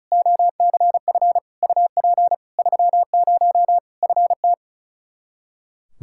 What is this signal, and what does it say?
Morse code, heard as a keyed tone, about 34 wpm, 700 Hz. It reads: OCF UP 30 FT
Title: Practice QSO - Exchange 2 - Info - 35wpm